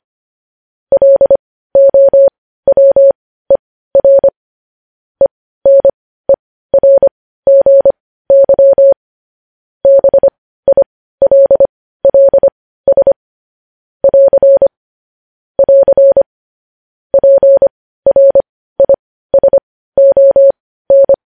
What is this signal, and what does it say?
Morse code, heard as a keyed tone, about 25 wpm, spaced as for 15 wpm, 550 Hz. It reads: LOWER ENERGY BILLS <AR> <AR> PRISON